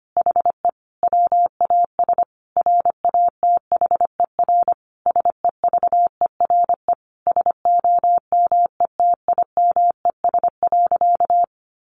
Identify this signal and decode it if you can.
Morse code, heard as a keyed tone, about 25 wpm, 700 Hz. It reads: HE WAS RATHER SEVERE SOMETIMES.
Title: Sentences from Top 1400 Words 25wpm